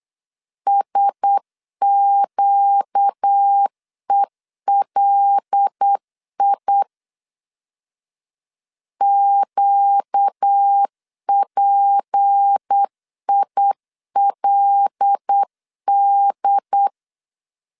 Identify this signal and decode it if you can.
Morse code, heard as a keyed tone, about 8 wpm, 800 Hz. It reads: SQELI QPILD
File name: LEZIONI / 6 PKX0 / 6 P 8wpm 3025